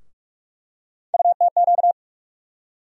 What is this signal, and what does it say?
Morse code, heard as a keyed tone, about 44 wpm, 700 Hz. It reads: UTQ
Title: USA Call Sign Postfixes - 3 Letters 45wpm